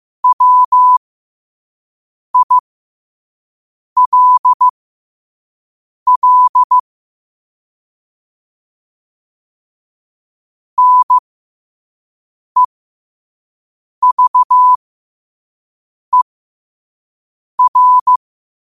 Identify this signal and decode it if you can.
Morse code, heard as a keyed tone, about 15 wpm, spaced as for 5 wpm, 1000 Hz. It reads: WILL NEVER
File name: qotdfiles / 5wpm